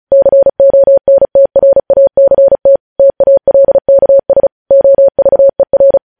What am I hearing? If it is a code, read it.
Morse code, heard as a keyed tone, about 36 wpm, 550 Hz. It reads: CONTRACT TALKS OVER